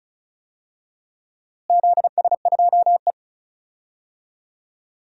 Morse code, heard as a keyed tone, about 36 wpm, 700 Hz. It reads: ZS2E
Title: DXCC-100 Call Signs - 4 Characters - Repeated 3x - 35wpm